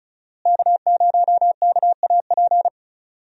Morse code, heard as a keyed tone, about 34 wpm, 700 Hz. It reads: K0KAP
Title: USA Call Signs - 5 Characters - Repeated 3x - 35wpm